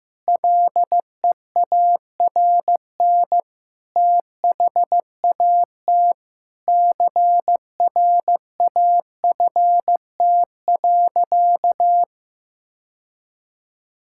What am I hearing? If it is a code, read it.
Morse code, heard as a keyed tone, about 15 wpm, 700 Hz. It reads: LEARN THAT CRAFT.
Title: Sentences from Top 1000 Words - Review Part 6 - 15wpm